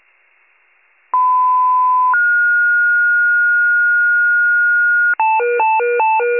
Two-tone page, heard as a one-second tone, then a three-second tone, about 1010 then 1430 Hz. It is heard from 1.1 to 5.1 s.
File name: pager / 94 FIRE / wcalert